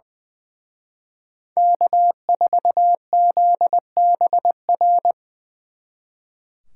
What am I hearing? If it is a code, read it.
Morse code, heard as a keyed tone, about 24 wpm, 700 Hz. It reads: K4ZBR
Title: Speed Racing - USA 1x3 Call Signs 20wpm